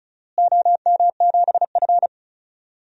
Morse code, heard as a keyed tone, about 34 wpm, 700 Hz. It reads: OM7F
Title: DXCC-100 Call Signs - 4 Characters - Repeated 2x - 35wpm